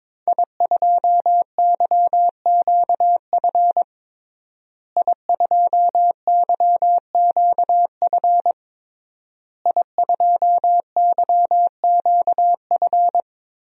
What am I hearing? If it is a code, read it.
Morse code, heard as a keyed tone, about 22 wpm, 700 Hz. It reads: I2YQF I2YQF I2YQF
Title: DXCC-100 Call Signs - 5 Characters - Repeated 3x - 22wpm